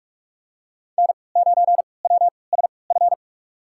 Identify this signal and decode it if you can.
Morse code, heard as a keyed tone, about 46 wpm, spaced as for 25 wpm, 700 Hz. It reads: N9WSF
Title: USA Call Signs - 5 Characters - 3x Character Spacing - 45wpm